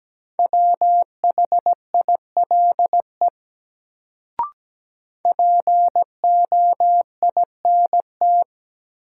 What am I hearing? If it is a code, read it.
Morse code, heard as a keyed tone, about 17 wpm, 700 Hz. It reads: WHILE POINT
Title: Sets of 3 Words - New Word Drill - 17wpm